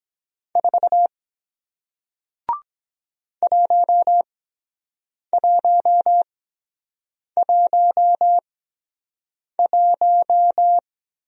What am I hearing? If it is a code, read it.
Morse code, heard as a keyed tone, about 23 wpm, spaced as for 7 wpm, 700 Hz. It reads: 4 1111